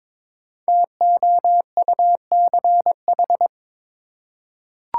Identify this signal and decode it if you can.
Morse code, heard as a keyed tone, about 22 wpm, 700 Hz. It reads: TOUCH
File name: 5 Letter Words - First 3 Letters Given - 22wpm